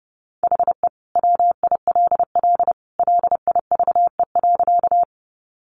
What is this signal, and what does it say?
Morse code, heard as a keyed tone, about 30 wpm, 700 Hz. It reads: HE WILL LIVE.
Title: Sentences from Top 200 Words 30wpm